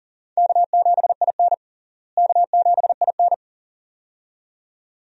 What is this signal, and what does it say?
Morse code, heard as a keyed tone, about 37 wpm, 700 Hz. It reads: K7IN K7IN
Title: USA Call Signs - 4 Characters - Repeated 2x - 40wpm